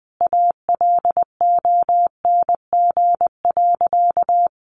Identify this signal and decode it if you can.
Morse code, heard as a keyed tone, about 20 wpm, 700 Hz. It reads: ALONG.
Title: Sentences from Top 700 Words 20wpm